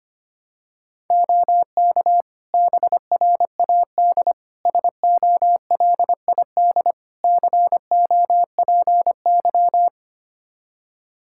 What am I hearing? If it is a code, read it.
Morse code, heard as a keyed tone, about 25 wpm, 700 Hz. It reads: OK BRAD SOLID COPY
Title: Practice QSO - Exchange 3 - More Info - 25wpm